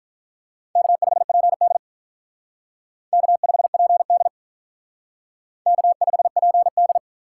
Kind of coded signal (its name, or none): Morse code